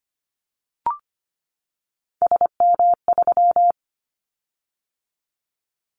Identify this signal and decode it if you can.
Morse code, heard as a keyed tone, about 25 wpm, 700 Hz. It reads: SM3
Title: Callsign Prefixes for Top 25 DXCC - 25wpm